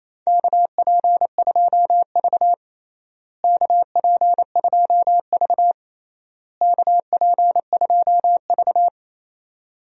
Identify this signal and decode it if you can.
Morse code, heard as a keyed tone, about 28 wpm, 700 Hz. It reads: KP2V KP2V KP2V
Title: DXCC-100 Call Signs - 4 Characters - Repeated 3x - 28wpm